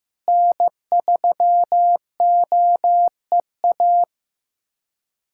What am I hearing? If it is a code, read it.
Morse code, heard as a keyed tone, about 15 wpm, 700 Hz. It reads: N3OEA
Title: USA Callsign DE Callsign 15wpm